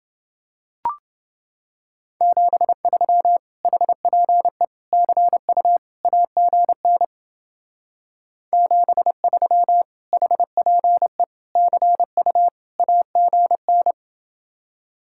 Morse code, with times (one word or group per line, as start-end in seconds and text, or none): 2.2-3.4 73
3.6-4.6 HPE
4.9-5.8 CU
6.0-7.0 AGN
8.5-9.8 73
10.1-11.2 HPE
11.6-12.5 CU
12.8-13.9 AGN